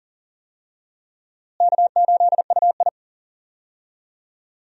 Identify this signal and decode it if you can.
Morse code, heard as a keyed tone, about 40 wpm, 700 Hz. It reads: K8UI
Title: USA Call Signs - 4 Characters - Repeated 3x - 40wpm